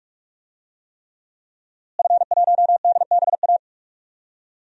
Morse code, heard as a keyed tone, about 46 wpm, 700 Hz. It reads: F1DBA
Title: DXCC-100 Call Signs - 5 Characters - Repeated 3x - 45wpm